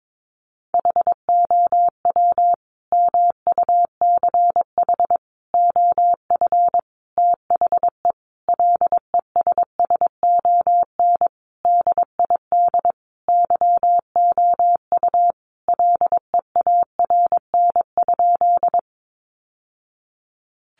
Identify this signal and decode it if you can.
Morse code, heard as a keyed tone, about 22 wpm, 700 Hz. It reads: HOW MUCH OF THE LESSON DID YOU LEARN?
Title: Sentences from Top 1200 Words 22wpm